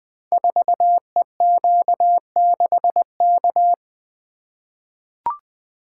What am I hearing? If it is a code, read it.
Morse code, heard as a keyed tone, about 20 wpm, 700 Hz. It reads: 4EQ6K
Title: Code Groups - 5 Letter:Numbers - 20wpm